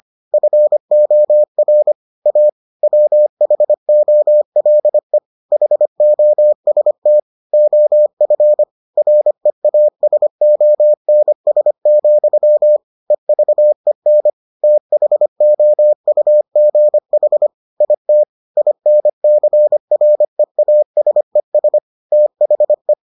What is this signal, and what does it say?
Morse code, heard as a keyed tone, about 25 wpm, 600 Hz. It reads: FOR A WHOLE HOST OF REASONS, EVEN THOUGH IT INCREASES THE